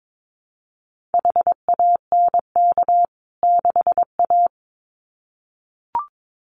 Morse code, heard as a keyed tone, about 22 wpm, 700 Hz. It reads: HANK 6A
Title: CWT Exchange 22wpm